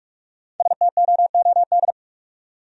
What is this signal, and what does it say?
Morse code, heard as a keyed tone, about 44 wpm, 700 Hz. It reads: STOOD